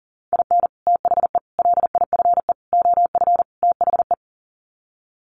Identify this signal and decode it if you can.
Morse code, heard as a keyed tone, about 40 wpm, 700 Hz. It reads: IN THE LIFE OF THE
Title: Sets of 5 Words Encore2 40wpm